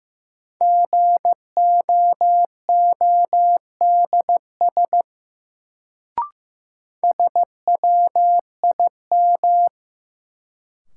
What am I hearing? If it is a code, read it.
Morse code, heard as a keyed tone, about 15 wpm, 700 Hz. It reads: GOODS SWIM
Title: Sentences from Top 1000 Words - Review Part 6 - New Word Drill - 15wpm